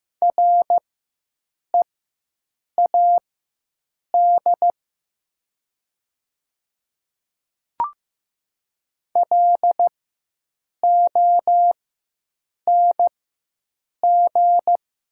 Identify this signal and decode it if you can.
Morse code, heard as a keyed tone, about 15 wpm, spaced as for 7 wpm, 700 Hz. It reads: READ LONG
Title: 4x Character Spacing - Top 200 Words - 15wpm